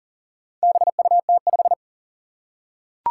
Morse code, heard as a keyed tone, about 40 wpm, 700 Hz. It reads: BUT5
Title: Four Challenging Letter-Number Combinations 40wpm